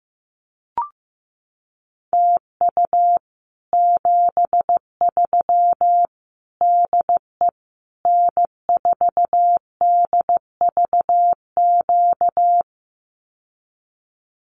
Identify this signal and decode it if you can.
Morse code, heard as a keyed tone, about 15 wpm, 700 Hz. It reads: TU 73 DE N4DVQ
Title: Common QSO Phrases Encore 15wpm